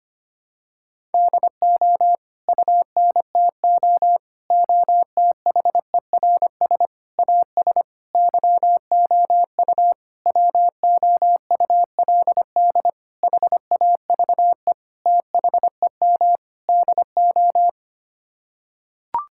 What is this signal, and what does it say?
Morse code, heard as a keyed tone, about 25 wpm, 700 Hz. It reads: DO UNTO OTHERS AS YOU WOULD HAVE THEM DO
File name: Sets of 9 Words - 25wpm